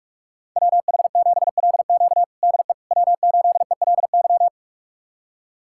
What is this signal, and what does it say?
Morse code, heard as a keyed tone, about 46 wpm, 700 Hz. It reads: WH7LQ DE W8ELY
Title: USA Callsign DE Callsign - 5 Characters - 45wpm